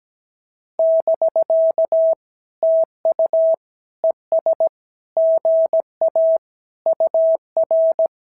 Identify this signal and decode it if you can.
Morse code, heard as a keyed tone, about 17 wpm, 650 Hz. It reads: <BK> TU ES GA UR
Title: SOTA QSO Practice 17wpm